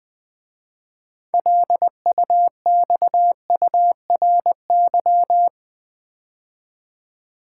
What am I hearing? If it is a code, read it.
Morse code, heard as a keyed tone, about 21 wpm, 700 Hz. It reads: LUXURY
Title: 6 Letter Words - First 3 Letters Given - 20wpm